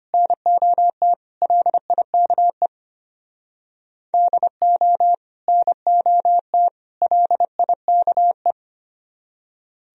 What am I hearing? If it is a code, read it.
Morse code, heard as a keyed tone, about 32 wpm, 700 Hz. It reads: NOT LIKE DO NOT LIKE